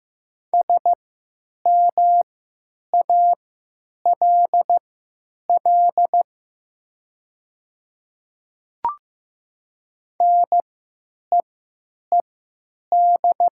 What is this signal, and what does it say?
Morse code, heard as a keyed tone, about 15 wpm, spaced as for 9 wpm, 700 Hz. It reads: SMALL NEED
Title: 3x Character Spacing - Top 200 Words - 15wpm